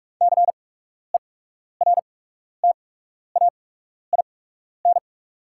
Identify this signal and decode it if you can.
Morse code, heard as a keyed tone, about 44 wpm, spaced as for 12 wpm, 690 Hz. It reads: CERTAIN
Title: 8x Character Spacing - Top 500 Words - 45wpm